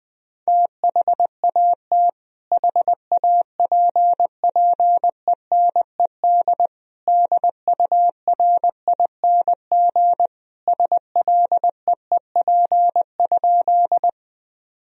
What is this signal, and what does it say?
Morse code, heard as a keyed tone, about 21 wpm, 700 Hz. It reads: THAT HAPPENED DURING SLEEP?
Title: Sentences from Top 1000 Words - Review Part 2 - 20wpm